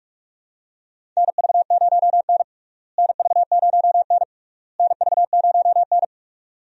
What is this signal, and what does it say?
Morse code, heard as a keyed tone, about 46 wpm, 700 Hz. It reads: NV0N NV0N NV0N
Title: USA Call Signs - 4 Characters - Repeated 3x - 45wpm